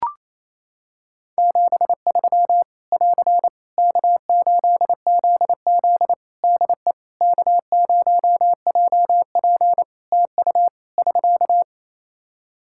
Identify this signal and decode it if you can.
Morse code, heard as a keyed tone, about 28 wpm, 700 Hz. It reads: 73 <AR> K8ZZ DE K0JP TU <SK>
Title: Practice QSO - Exchange 4 - Ending - 28wpm